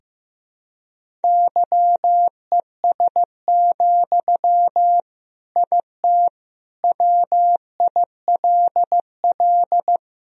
Morse code, heard as a keyed tone, about 15 wpm, 700 Hz. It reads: YES, IT WILL
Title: Sentences from Top 1000 Words - Review Part 1 - 15wpm